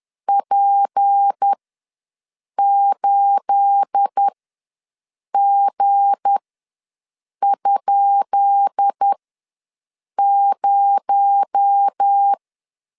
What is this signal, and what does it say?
Morse code, heard as a keyed tone, about 10 wpm, spaced as for 6 wpm, 800 Hz. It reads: P8G?0
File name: LEZIONI / 10 SIGNS / 10B INT ws 10wpm 4134